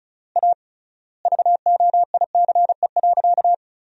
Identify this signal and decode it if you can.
Morse code, heard as a keyed tone, about 34 wpm, 700 Hz. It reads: A VOICE.
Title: 3x Word-Spacing - Sentences from Top 500 Words - 35wpm